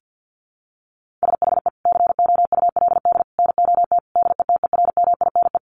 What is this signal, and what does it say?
Morse code, heard as a keyed tone, about 50 wpm, 700 Hz. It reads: SHE COULD NOT DETERMINE
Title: Sentences from Top 700 Words 50wpm